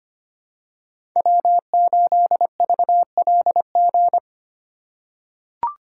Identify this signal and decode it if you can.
Morse code, heard as a keyed tone, about 25 wpm, 700 Hz. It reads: W8VLG
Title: USA Call Signs - 5 Characters - 4x Character Spacing - 25wpm